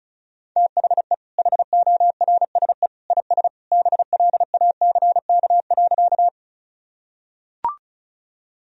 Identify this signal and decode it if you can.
Morse code, heard as a keyed tone, about 36 wpm, 700 Hz. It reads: THE HORSE IS BLACK.
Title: Sentences from Top 1000 Words - Review Part 2 - 35wpm